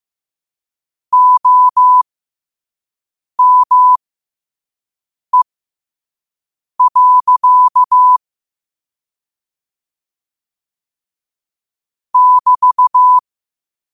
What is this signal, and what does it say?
Morse code, heard as a keyed tone, about 15 wpm, spaced as for 5 wpm, 1000 Hz. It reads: OME. <BT>
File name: qotdfiles / 5wpm